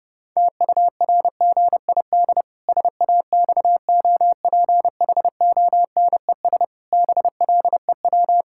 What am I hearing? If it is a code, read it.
Morse code, heard as a keyed tone, about 30 wpm, 700 Hz. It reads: TURGID SAXOPHONES BLEW
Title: Pangram Challenge v2 - 30wpm